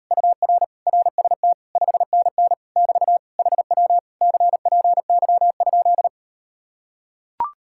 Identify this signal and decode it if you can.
Morse code, heard as a keyed tone, about 39 wpm, 700 Hz. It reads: UR RST 5NN <BT> HW CPY?